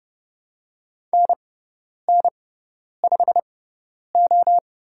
Morse code, heard as a keyed tone, about 30 wpm, spaced as for 10 wpm, 700 Hz. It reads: NN5O
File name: Lesson 17g - Letter u - Call Signs 30wpm